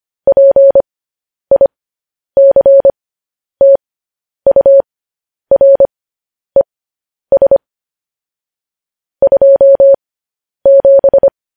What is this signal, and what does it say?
Morse code, heard as a keyed tone, about 25 wpm, spaced as for 10 wpm, 550 Hz. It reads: PICTURES 27